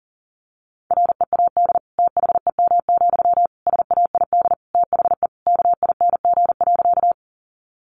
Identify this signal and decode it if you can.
Morse code, heard as a keyed tone, about 40 wpm, 700 Hz. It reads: READ THEM, SAID THE KING.
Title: Sentences from Top 400 Words 40wpm